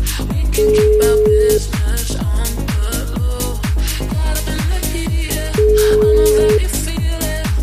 Ringback tone (Europe, 425 Hz), with noise.